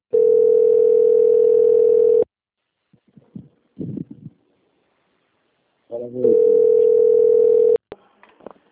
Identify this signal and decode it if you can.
Ringback tone (North America), 440 + 480 Hz.